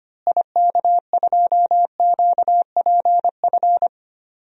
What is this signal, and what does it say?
Morse code, heard as a keyed tone, about 25 wpm, 700 Hz. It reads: IK2QPF
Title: DXCC-100 Call Signs - 6 Characters - Repeated 2x - 25wpm